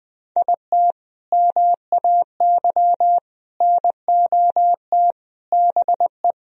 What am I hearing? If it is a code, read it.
Morse code, heard as a keyed tone, about 21 wpm, 700 Hz. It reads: IT MAY NOT BE
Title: Sentences from Top 1000 Words - Review Part 4 - 20wpm